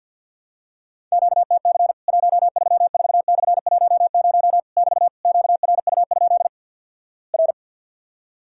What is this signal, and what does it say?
Morse code, heard as a keyed tone, about 50 wpm, 700 Hz. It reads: QTC 134/10 <BT> QRU? R
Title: Worked All Europe DX Contest - QTC Exchange with Repeats - 50wpm